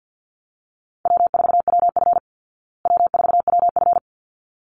Morse code, heard as a keyed tone, about 49 wpm, 700 Hz. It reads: W4WF W4WF
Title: Easy Call Signs Repeated 2x - 50wpm